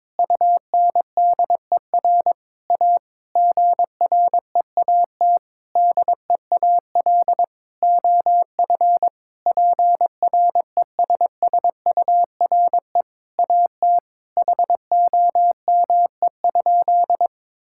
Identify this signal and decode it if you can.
Morse code, heard as a keyed tone, about 22 wpm, 700 Hz. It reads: UNDER A GREAT DEAL OF PRESSURE AT HOME?
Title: Sentences from Top 1000 Words - Review Part 6 - 22wpm